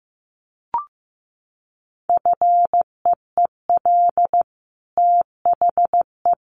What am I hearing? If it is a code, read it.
Morse code, heard as a keyed tone, about 15 wpm, 700 Hz. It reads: FEEL THE
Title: Sentences from Top 1400 Words 15wpm